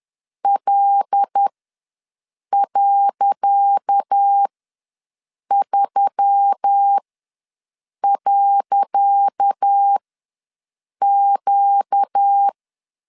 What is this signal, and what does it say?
Morse code, heard as a keyed tone, about 10 wpm, spaced as for 6 wpm, 800 Hz. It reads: L.3.Q
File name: LEZIONI / 10 SIGNS / 10C DOT ws 10wpm 4213